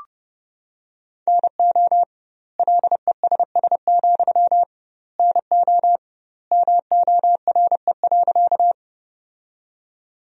Morse code, heard as a keyed tone, about 30 wpm, 700 Hz. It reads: NO LESS, NO MORE.
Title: 2x Word-Spacing - Sentences from Top 400 Words - 30wpm